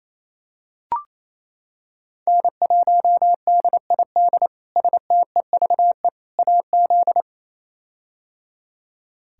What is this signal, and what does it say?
Morse code, heard as a keyed tone, about 28 wpm, 700 Hz. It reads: N1DID STEVE AZ
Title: K1USN SST QSO Practice - 28wpm